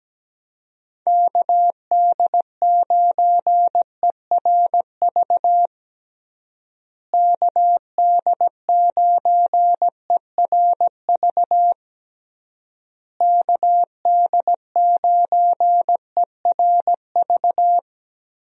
Morse code, heard as a keyed tone, about 17 wpm, 700 Hz. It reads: KD9ERV KD9ERV KD9ERV
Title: USA Call Signs - 6 Characters - Repeated 3x - 17wpm